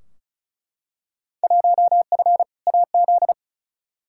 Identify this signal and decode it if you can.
Morse code, heard as a keyed tone, about 36 wpm, 700 Hz. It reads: 1F AZ
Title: Field Day Exchanges - 35wpm